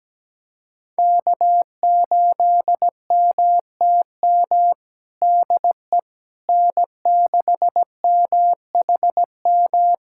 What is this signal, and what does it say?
Morse code, heard as a keyed tone, about 17 wpm, 700 Hz. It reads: K8MTM DE N6MHM
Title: USA Callsign DE Callsign - 5 Characters - 17wpm